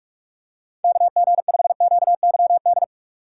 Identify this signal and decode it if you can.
Morse code, heard as a keyed tone, about 46 wpm, 700 Hz. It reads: KG5QYD
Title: USA Call Signs - 6 Characters - 8x Character Spacing - 45wpm